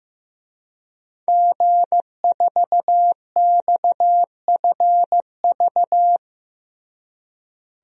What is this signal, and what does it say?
Morse code, heard as a keyed tone, about 15 wpm, 700 Hz. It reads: G4XFV
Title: DXCC-100 Call Signs - 5 Characters - Repeated 3x - 15wpm